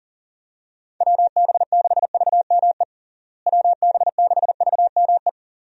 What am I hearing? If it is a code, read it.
Morse code, heard as a keyed tone, about 40 wpm, 700 Hz. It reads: WB6VME WB6VME
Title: USA Call Signs - 6 Characters - Repeated 2x - 40wpm